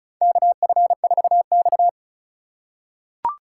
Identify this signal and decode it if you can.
Morse code, heard as a keyed tone, about 36 wpm, 700 Hz. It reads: KF4X
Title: USA Call Signs - 4 Characters - 4x Character Spacing - 35wpm